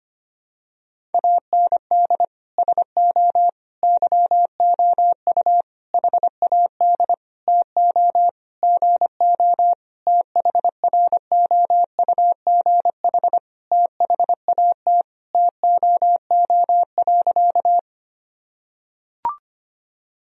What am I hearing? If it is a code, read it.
Morse code, heard as a keyed tone, about 25 wpm, 700 Hz. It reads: AND SO YOU HAD TO GO THROUGH THAT TOO.
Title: Sentences from Top 1000 Words - Review Part 1 - 25wpm